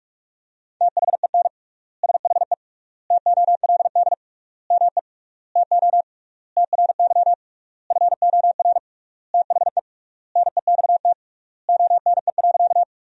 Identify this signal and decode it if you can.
Morse code, heard as a keyed tone, about 41 wpm, 700 Hz. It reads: THEN SHE TOLD ME TO TRY FOR THE NEXT ONE.
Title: 3x Word-Spacing - Sentences from Top 400 Words - 45wpm